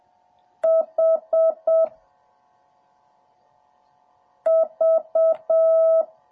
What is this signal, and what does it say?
Morse code, heard as a keyed tone, about 7 wpm, 650 Hz. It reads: H V